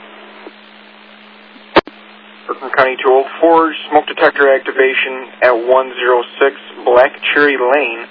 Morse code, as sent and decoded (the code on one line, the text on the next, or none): none